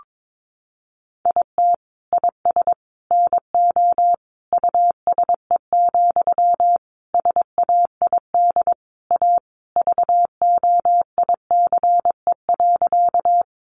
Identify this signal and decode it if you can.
Morse code, heard as a keyed tone, about 22 wpm, 700 Hz. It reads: IT IS NO USE, SAID A VOICE.
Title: Sentences from Top 500 Words 22wpm